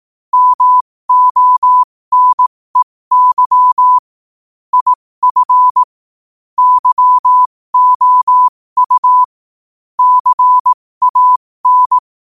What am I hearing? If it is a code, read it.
Morse code, heard as a keyed tone, about 18 wpm, spaced as for 15 wpm, 1000 Hz. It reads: MONEY IF YOU CAN
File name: qotdfiles / 15wpm